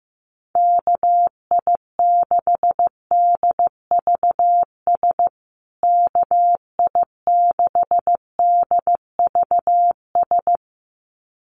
Morse code, heard as a keyed tone, about 15 wpm, 700 Hz. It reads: KI6DVS KI6DVS
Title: USA Call Signs Repeated 2x 15wpm